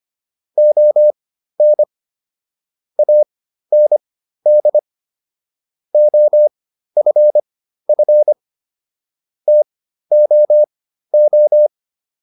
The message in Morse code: ON AND OFF TOO

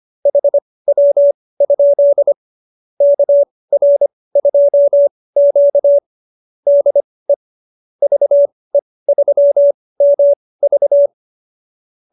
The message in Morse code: HW? KR2Q DE VE3MV

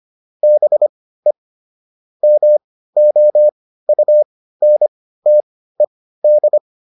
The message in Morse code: BE MOUNTED

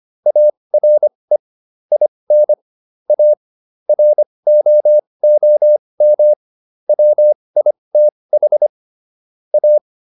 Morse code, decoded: ARE IN A ROOM WITH A

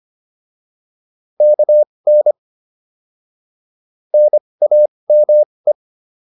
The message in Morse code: KN NAME

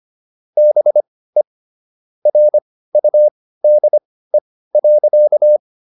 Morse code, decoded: BE RUDE.

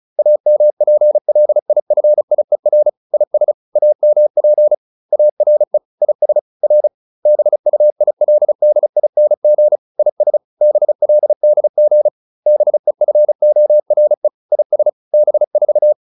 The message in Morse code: AMPLIFIER IS AMP ARE IS R BUILDING IS BLDG BEFORE IS B4

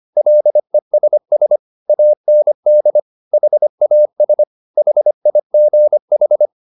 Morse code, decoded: LESS AND HAS HIGH

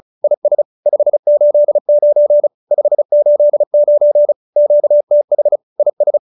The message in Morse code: IS 589 589 QTH IS

